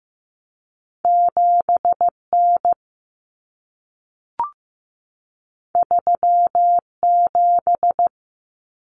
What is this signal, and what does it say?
Morse code, heard as a keyed tone, about 15 wpm, 700 Hz. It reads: 7N 37
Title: Common Cut Numbers - 2-Digit 15wpm